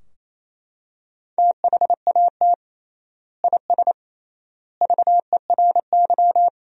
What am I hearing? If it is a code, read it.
Morse code, heard as a keyed tone, about 28 wpm, 700 Hz. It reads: THAT IS VERY
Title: 3x Word-Spacing - Sentences from Top 400 Words - 28wpm